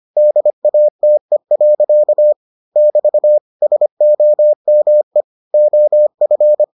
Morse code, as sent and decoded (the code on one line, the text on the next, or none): -.. .- - . .-.-.- / -...- / ... --- -- . / --- ..-.
DATE. <BT> SOME OF